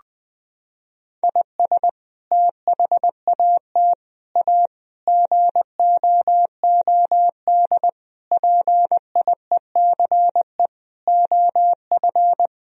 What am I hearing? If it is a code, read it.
Morse code, heard as a keyed tone, about 20 wpm, 700 Hz. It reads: IS THAT A GOOD PIECE OF
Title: Sentences from Top 1000 Words - Review Part 3 - 20wpm